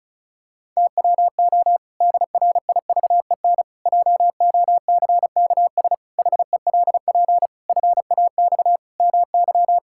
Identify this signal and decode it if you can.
Morse code, heard as a keyed tone, about 36 wpm, 700 Hz. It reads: TWO DRIVEN JOCKS HELP FAX MY
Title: Pangram Challenge v2 - 35wpm